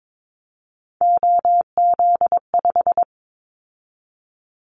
Morse code, heard as a keyed tone, about 22 wpm, 700 Hz. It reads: OZ5
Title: Callsign Prefixes for Top 50 DXCC - 22wpm